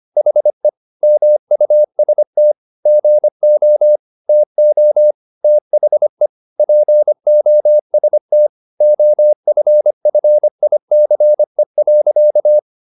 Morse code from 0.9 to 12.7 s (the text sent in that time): MUST GO TO THE POST OFFICE.